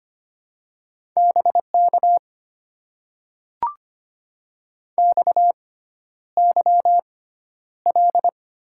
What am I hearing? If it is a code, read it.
Morse code, heard as a keyed tone, about 25 wpm, spaced as for 11 wpm, 700 Hz. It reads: BK XYL
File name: Top 90 QSO Elements - 6x Character Spacing - 25wpm